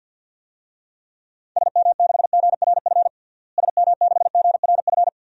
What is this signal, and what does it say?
Morse code, heard as a keyed tone, about 53 wpm, 700 Hz. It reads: SM6GRF SM6GRF
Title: DXCC-100 Call Signs - 6 Characters - Repeated 2x - 50wpm